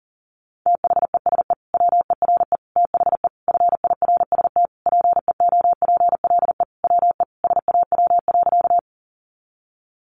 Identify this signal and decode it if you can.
Morse code, heard as a keyed tone, about 37 wpm, 700 Hz. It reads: THESE WERE THE FIRST PEOPLE WE SAW.
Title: Sentences from Top 300 Words 40wpm